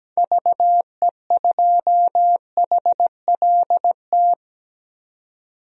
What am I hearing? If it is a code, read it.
Morse code, heard as a keyed tone, about 17 wpm, 700 Hz. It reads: VE2HLT
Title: DXCC-100 Call Signs - 6 Characters - 17wpm